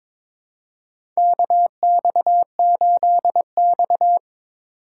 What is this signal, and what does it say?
Morse code, heard as a keyed tone, about 22 wpm, 700 Hz. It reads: KX8X
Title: K1USN SST QSO Practice - 22wpm